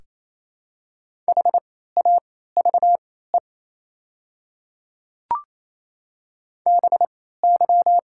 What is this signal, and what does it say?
Morse code, heard as a keyed tone, about 28 wpm, spaced as for 16 wpm, 700 Hz. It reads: HAVE BY
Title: 3x Character Spacing - Top 100 Words - 28wpm